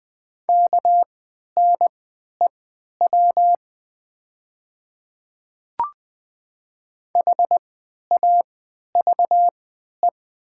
Morse code, heard as a keyed tone, about 20 wpm, spaced as for 11 wpm, 700 Hz. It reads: KNEW HAVE
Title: 3x Character Spacing - Top 500 Words - 20wpm